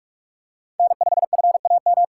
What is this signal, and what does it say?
Morse code, heard as a keyed tone, about 41 wpm, 700 Hz. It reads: N5FAM